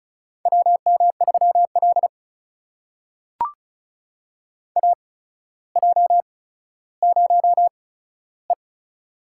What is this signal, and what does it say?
Morse code, heard as a keyed tone, about 36 wpm, spaced as for 15 wpm, 700 Hz. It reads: WM3L AJ0E
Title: USA Call Signs - 4 Characters - 8x Character Spacing - 35wpm